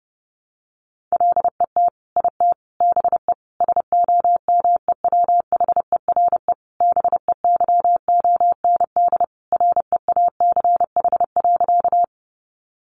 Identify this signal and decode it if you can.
Morse code, heard as a keyed tone, about 30 wpm, 700 Hz. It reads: LET IT BE SOMEWHERE BEYOND REACH.
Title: Sentences from Top 1400 Words 30wpm